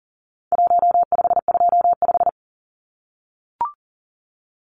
Morse code, heard as a keyed tone, about 37 wpm, 700 Hz. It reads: 1525
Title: Numbers - Contest Style - 4-Digit 40wpm